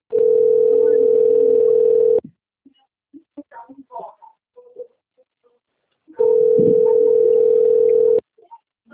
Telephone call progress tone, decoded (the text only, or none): ringback tone